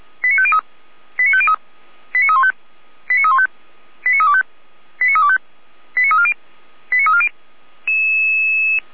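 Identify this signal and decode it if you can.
Five-tone selective call (ZVEI), heard as quick runs of five tones, tones of about 70 ms. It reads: 89503 89503 89315 89315 89325 89325 89340 89340
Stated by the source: ZVEI five-tone selective call received on an SDR.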